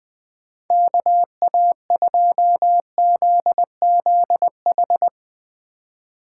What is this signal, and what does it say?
Morse code, heard as a keyed tone, about 21 wpm, 700 Hz. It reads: KA2ZZH
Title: Speed Racing - USA 2x3 Call Signs 17wpm